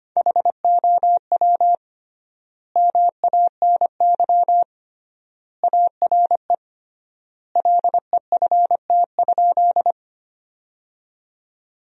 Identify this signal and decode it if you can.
Morse code, heard as a keyed tone, about 25 wpm, 700 Hz. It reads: HOW MANY ARE LEFT?
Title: 3x Word-Spacing - Sentences from Top 300 Words - 25wpm